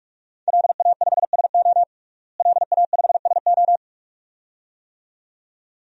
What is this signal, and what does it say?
Morse code, heard as a keyed tone, about 46 wpm, 700 Hz. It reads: LA5SO LA5SO
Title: DXCC-100 Call Signs - 5 Characters - Repeated 2x - 45wpm